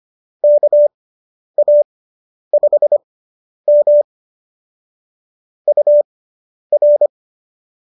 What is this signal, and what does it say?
Morse code, heard as a keyed tone, about 25 wpm, spaced as for 10 wpm, 590 Hz. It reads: KA5M UR